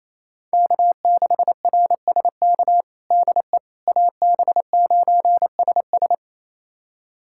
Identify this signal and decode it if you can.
Morse code, heard as a keyed tone, about 28 wpm, 700 Hz. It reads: K6RSK DE AB9SS
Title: USA Callsign DE Callsign - 5 Characters - 28wpm